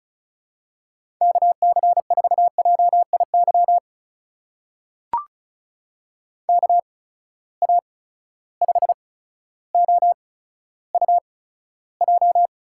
Morse code, heard as a keyed tone, about 34 wpm, spaced as for 15 wpm, 700 Hz. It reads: KC4JIY KA5OUJ
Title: USA Call Signs - 6 Characters - 8x Character Spacing - 35wpm